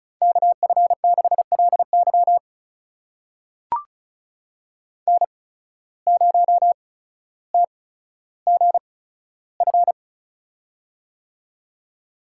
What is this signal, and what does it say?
Morse code, heard as a keyed tone, about 36 wpm, spaced as for 15 wpm, 700 Hz. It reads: KF6LY N0TGF
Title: USA Call Signs - 5 Characters - 8x Character Spacing - 35wpm